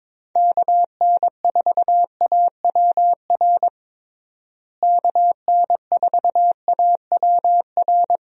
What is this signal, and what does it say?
Morse code, heard as a keyed tone, about 21 wpm, 700 Hz. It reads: KN4AWR KN4AWR
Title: USA Call Signs - 6 Characters - Repeated 2x - 22wpm